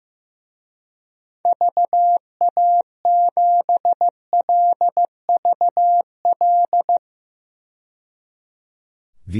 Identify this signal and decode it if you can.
Morse code, heard as a keyed tone, about 15 wpm, 700 Hz. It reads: VA7LVL
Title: DXCC-100 Call Signs - 6 Characters - 15wpm